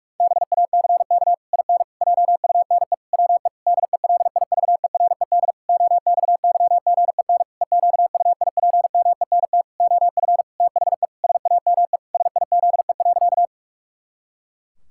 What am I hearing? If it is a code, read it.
Morse code, heard as a keyed tone, about 41 wpm, 700 Hz. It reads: BACK IN JUNE WE DELIVERED OXYGEN EQUIPMENT OF THE SAME SIZE.
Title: Pangram Challenge v2 - 45wpm